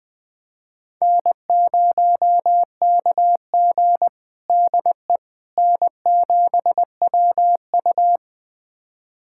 Morse code, heard as a keyed tone, about 20 wpm, 700 Hz. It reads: N0KG DE N7WU
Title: USA Callsign DE Callsign - 4 Characters - 20wpm